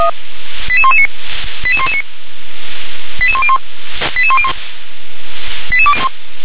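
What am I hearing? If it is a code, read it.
Five-tone selective call (ZVEI), heard as quick runs of five tones, tones of about 70 ms. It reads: 88119 88119 88101 88101 88201
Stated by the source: ZVEI five-tone selective call received on an SDR.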